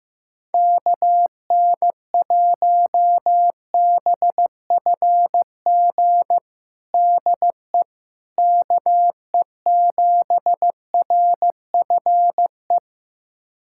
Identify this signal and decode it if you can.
Morse code, heard as a keyed tone, about 16 wpm, 700 Hz. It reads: KN1BFG DE KE7RFE
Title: USA Callsign DE Callsign - 6 Characters - 15wpm